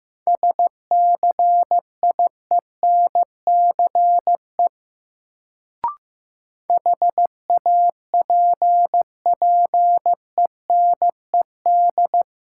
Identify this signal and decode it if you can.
Morse code, heard as a keyed tone, about 15 wpm, 700 Hz. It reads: SCIENCE HAPPENED
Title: Sentences from Top 1000 Words - Review Part 2 - New Word Drill - 15wpm